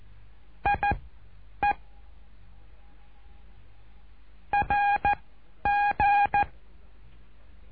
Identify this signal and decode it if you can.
Morse code, heard as a keyed tone, about 14 wpm, spaced as for 9 wpm, 820 Hz. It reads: IE RG
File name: cursos / CURSO DE TELEGRAFIA / Telegrafia - Trilha 08